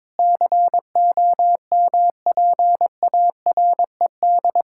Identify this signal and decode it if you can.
Morse code, heard as a keyed tone, about 22 wpm, 700 Hz. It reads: COMPARED
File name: practice / 22wpm / Sets of 7 Words - 22wpm